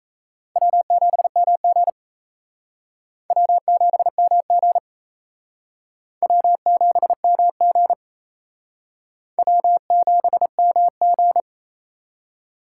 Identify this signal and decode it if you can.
Morse code, heard as a keyed tone, about 35 wpm, 700 Hz. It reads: W7MG W7MG W7MG W7MG